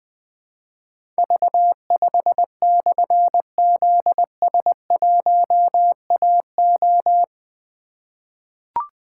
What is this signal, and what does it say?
Morse code, heard as a keyed tone, about 20 wpm, 700 Hz. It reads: V5/ZS1AO
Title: Difficult Call Signs 20wpm